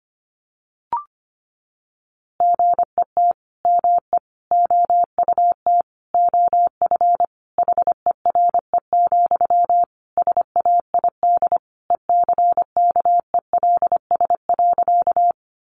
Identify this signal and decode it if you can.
Morse code, heard as a keyed tone, about 25 wpm, 700 Hz. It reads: GET ME OUT OF HERE, SAID ECKELS.
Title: A Sound Of Thunder 25wpm